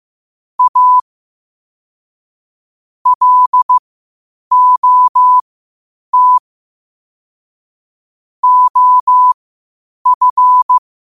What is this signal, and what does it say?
Morse code, heard as a keyed tone, about 15 wpm, spaced as for 8 wpm, 1000 Hz. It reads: A LOT OF
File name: qotdfiles / 8wpm